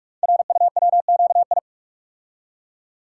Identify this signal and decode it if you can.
Morse code, heard as a keyed tone, about 44 wpm, 700 Hz. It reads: RUWQI
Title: Code Groups - 5 Letters - 45wpm